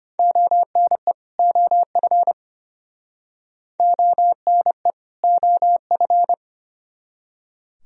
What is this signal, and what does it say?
Morse code, heard as a keyed tone, about 33 wpm, 700 Hz. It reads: ONE OF ONE OF